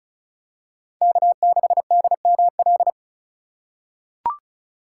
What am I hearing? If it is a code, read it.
Morse code, heard as a keyed tone, about 34 wpm, 700 Hz. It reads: K6DML
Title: USA Call Signs - 4x Character Spacing - 35wpm